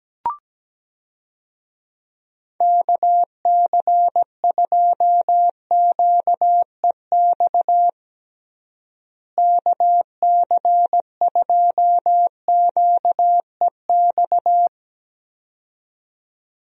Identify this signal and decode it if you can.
Morse code, heard as a keyed tone, about 17 wpm, 700 Hz. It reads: KC2QEX KC2QEX
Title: USA Call Signs - 6 Characters - Repeated 2x - 17wpm